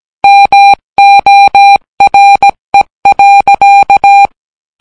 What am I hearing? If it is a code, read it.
Morse code, heard as a keyed tone, about 17 wpm, 800 Hz. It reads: MORE.